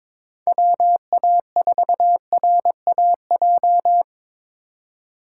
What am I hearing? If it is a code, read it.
Morse code, heard as a keyed tone, about 26 wpm, 700 Hz. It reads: WA4RAJ
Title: Speed Racing - USA Call Signs 22wpm